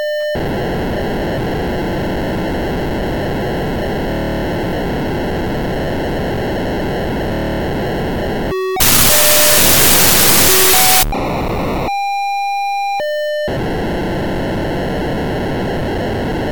POCSAG pager data.